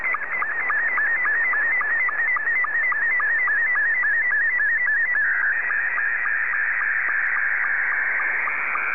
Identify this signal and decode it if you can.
SSTV (slow-scan television), Scottie 2.
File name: sstv / S2-VK3HJV